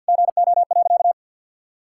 Morse code, heard as a keyed tone, about 50 wpm, 700 Hz. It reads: GO.
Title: Sentences from Top 1600 Words 50wpm